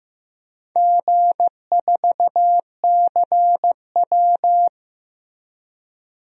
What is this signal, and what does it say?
Morse code, heard as a keyed tone, about 15 wpm, 700 Hz. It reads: G4CW
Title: DXCC-100 Call Signs - 4 Characters - 15wpm